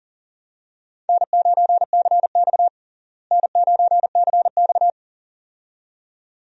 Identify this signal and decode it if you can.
Morse code, heard as a keyed tone, about 40 wpm, 700 Hz. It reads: N9CX N9CX
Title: USA Call Signs - 4 Characters - Repeated 2x - 40wpm